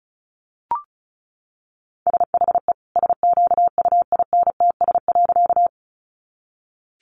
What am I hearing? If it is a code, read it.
Morse code, heard as a keyed tone, about 34 wpm, 700 Hz. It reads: SHE SQUINTS.
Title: Additional Easy Phrases 35wpm